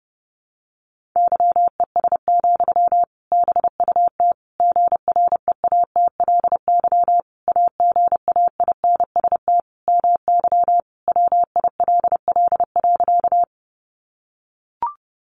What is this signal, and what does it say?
Morse code, heard as a keyed tone, about 30 wpm, 700 Hz. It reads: YES, BUT GREATLY AGAINST MY WILL.
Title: Sentences from Top 1400 Words 30wpm